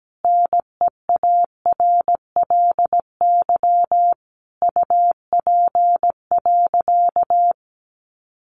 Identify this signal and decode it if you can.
Morse code, heard as a keyed tone, about 17 wpm, 700 Hz. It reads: NEARLY UP.
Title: Sentences from Top 1300 Words 17wpm